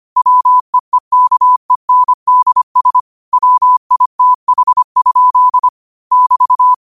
Morse code, heard as a keyed tone, about 25 wpm, 1000 Hz. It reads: WEEKENDS WITH? <BT>
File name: qotdfiles / 25wpm